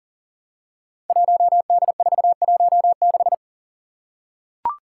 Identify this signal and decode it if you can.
Morse code, heard as a keyed tone, about 42 wpm, 690 Hz. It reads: 1D416
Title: Five Challenging Letter-Number Combinations 40wpm